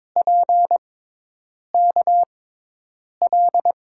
Morse code, heard as a keyed tone, about 22 wpm, spaced as for 8 wpm, 700 Hz. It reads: PKL
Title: USA Call Signs - 5 Characters - 6x Character Spacing - 22wpm